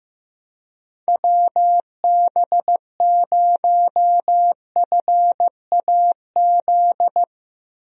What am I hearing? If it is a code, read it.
Morse code, heard as a keyed tone, about 15 wpm, 700 Hz. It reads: WB0FAZ
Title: USA Call Signs - 6 Characters - 15wpm